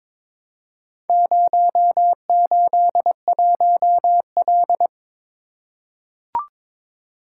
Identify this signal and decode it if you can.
Morse code, heard as a keyed tone, about 22 wpm, 700 Hz. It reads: 081L
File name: practice / 22wpm / Four Letter-Number Combinations - 22wpm